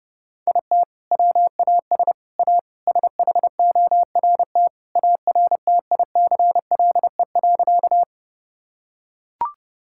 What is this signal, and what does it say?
Morse code, heard as a keyed tone, about 30 wpm, 700 Hz. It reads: IT WAS A SHORT ARTICLE.
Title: Sentences from Top 1000 Words - Review Part 5 - 30wpm